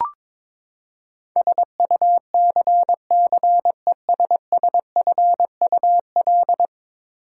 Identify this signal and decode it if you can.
Morse code, heard as a keyed tone, about 22 wpm, 700 Hz. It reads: SUCCESSFUL
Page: morse-code-ninja-practice-22wpm